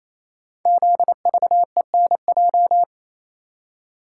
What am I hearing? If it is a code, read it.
Morse code, heard as a keyed tone, about 27 wpm, 700 Hz. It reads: ZVENJ
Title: Code Groups - 5 Letters - 28wpm